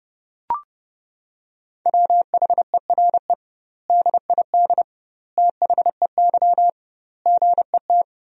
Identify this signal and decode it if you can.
Morse code, heard as a keyed tone, about 30 wpm, 700 Hz. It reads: WHERE DID THEY GET
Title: 2x Word-Spacing - Sentences from Top 400 Words - 30wpm